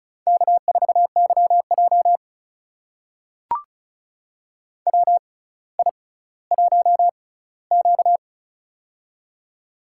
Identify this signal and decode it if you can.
Morse code, heard as a keyed tone, about 36 wpm, spaced as for 18 wpm, 700 Hz. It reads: K4YJ WI1Q
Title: USA Call Signs - 4 Characters - 6x Character Spacing - 35wpm